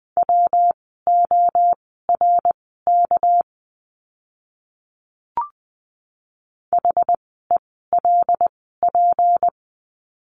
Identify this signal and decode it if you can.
Morse code, heard as a keyed tone, about 20 wpm, spaced as for 15 wpm, 700 Hz. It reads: WORK HELP
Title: 237 Lesson 38f - Letter q - Words 20wpm